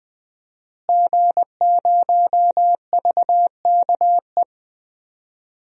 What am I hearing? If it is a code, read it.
Morse code, heard as a keyed tone, about 21 wpm, 700 Hz. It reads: G0VKE
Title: DXCC-100 Call Signs - 5 Characters - Repeated 3x - 20wpm